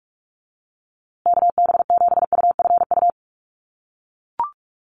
Morse code, heard as a keyed tone, about 44 wpm, 700 Hz. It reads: KB7UFU
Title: USA Call Signs Repeated 3x 45wpm